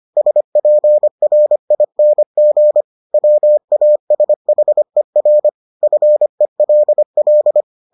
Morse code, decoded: SPRING WASHER FELL

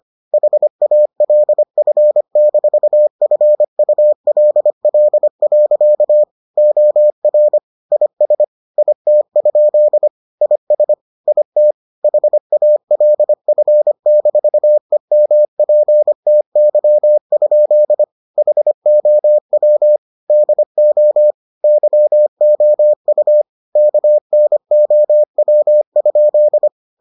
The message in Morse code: HALF-FULL. OR IS IT? IS IT HALF-EMPTY? HOW DO YOU KNOW?

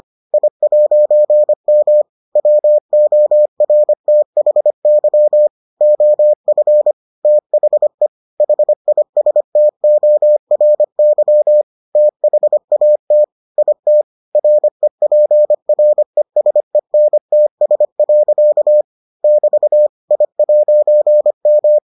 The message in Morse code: I'M WORTHY OF THE HISTORY THAT IT REPRESENTS. <BT> I'M

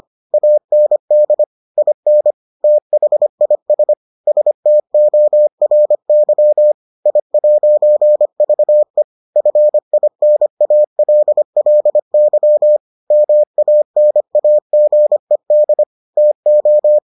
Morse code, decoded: AND IN THIS STORY I'VE FINALLY MANAGED TO

